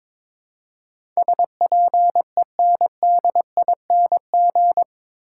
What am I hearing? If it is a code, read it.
Morse code, heard as a keyed tone, about 22 wpm, 700 Hz. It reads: SPENDING
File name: practice / 22wpm / Eight Letter Words - 22wpm